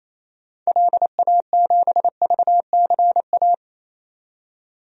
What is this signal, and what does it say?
Morse code, heard as a keyed tone, about 27 wpm, 700 Hz. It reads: LA7VCA
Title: DXCC-100 Call Signs - 6 Characters - 28wpm